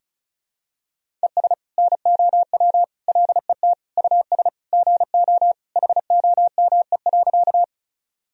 Morse code, heard as a keyed tone, about 36 wpm, 700 Hz. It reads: ES NOW LET US GO HOME.
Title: Sentences from Top 300 Words and Abbreviations 35wpm